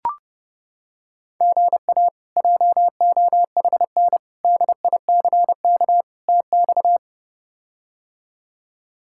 Morse code, heard as a keyed tone, about 32 wpm, 700 Hz. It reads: GA JOHN DICK TX
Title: K1USN SST QSO Practice - 30wpm